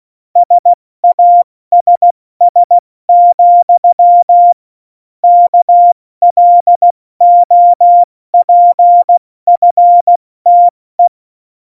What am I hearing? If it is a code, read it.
Morse code, heard as a keyed tone, about 16 wpm, 700 Hz. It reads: SASS, KLOPFTE
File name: Morsekurs / Witze / Witz 10. Oktober 17, Tempo 70